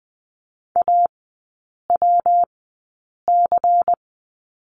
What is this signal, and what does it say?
Morse code, heard as a keyed tone, about 20 wpm, spaced as for 8 wpm, 700 Hz. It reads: AWC
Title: 166 Lesson 27e - Number 7 - Three Characters 20wpm